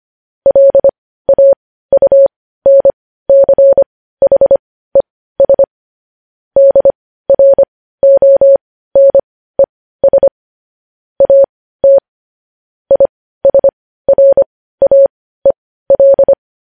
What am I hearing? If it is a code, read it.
Morse code, heard as a keyed tone, about 25 wpm, spaced as for 15 wpm, 550 Hz. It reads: LAUNCHES DRONES AT ISRAEL